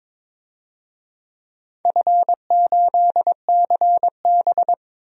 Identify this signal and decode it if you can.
Morse code, heard as a keyed tone, about 21 wpm, 700 Hz. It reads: F8CB
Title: DXCC-100 Call Signs - 4 Characters - 22wpm